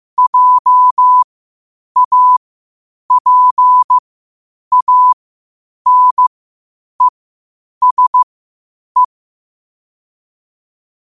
Morse code, heard as a keyed tone, about 15 wpm, spaced as for 8 wpm, 1000 Hz. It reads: JAPANESE
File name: qotdfiles / 8wpm